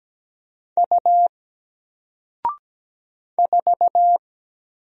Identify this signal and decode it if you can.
Morse code, heard as a keyed tone, about 17 wpm, 700 Hz. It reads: U 4
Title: Single Character (Letter:Number:Punctuation:Prosign) Alt 17wpm